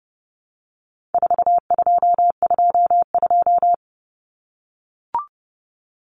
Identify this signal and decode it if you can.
Morse code, heard as a keyed tone, about 30 wpm, 700 Hz. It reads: 4222
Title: Common Cut Numbers - 4-Digit 30wpm